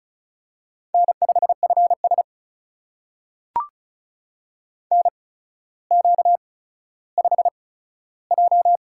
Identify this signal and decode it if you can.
Morse code, heard as a keyed tone, about 36 wpm, spaced as for 15 wpm, 700 Hz. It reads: N5FS NQ5J
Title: USA Call Signs - 4 Characters - 8x Character Spacing - 35wpm